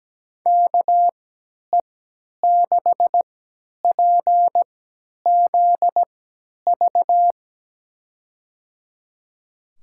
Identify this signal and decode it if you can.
Morse code, heard as a keyed tone, about 17 wpm, spaced as for 10 wpm, 700 Hz. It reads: KE6PZV
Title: USA Call Signs - 3x Character Spacing - 17wpm